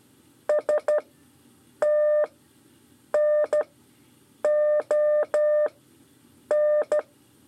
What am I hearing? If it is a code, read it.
Morse code, heard as a keyed tone, about 14 wpm, 600 Hz. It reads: S T N O N
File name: 5 LEZIONE RIPASSO 2